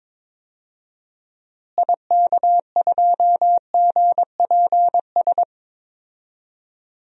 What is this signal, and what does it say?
Morse code, heard as a keyed tone, about 22 wpm, 700 Hz. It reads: IK2GPS
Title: DXCC-100 Call Signs - 6 Characters - Repeated 2x - 22wpm